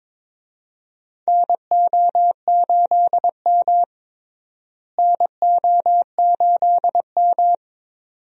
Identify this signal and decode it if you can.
Morse code, heard as a keyed tone, about 22 wpm, 700 Hz. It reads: NO8M NO8M
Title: USA Call Signs - 4 Characters - Repeated 2x - 22wpm